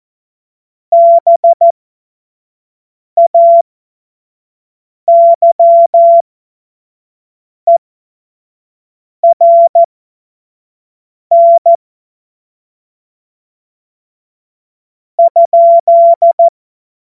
Morse code, heard as a keyed tone, about 14 wpm, spaced as for 4 wpm, 680 Hz. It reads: BAYERN?
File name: Witz-14-Oktober-2025-25-F70